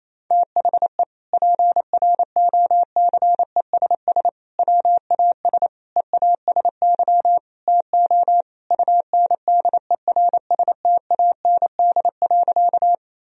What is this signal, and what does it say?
Morse code, heard as a keyed tone, about 28 wpm, 700 Hz. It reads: THE PROCESS WAS EASY TO UNDERSTAND.
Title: Sentences from Top 1000 Words - Review Part 3 - 28wpm